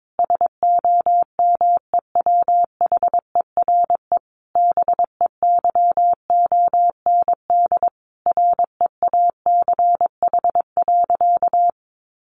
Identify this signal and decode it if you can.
Morse code, heard as a keyed tone, about 22 wpm, 700 Hz. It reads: SOMEWHERE BEYOND REACH.
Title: Sentences from Top 1400 Words 22wpm